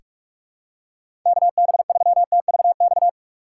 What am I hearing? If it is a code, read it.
Morse code, heard as a keyed tone, about 46 wpm, 700 Hz. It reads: KB3TVX
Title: USA Call Signs - 6 Characters - 2x Character Spacing - 45wpm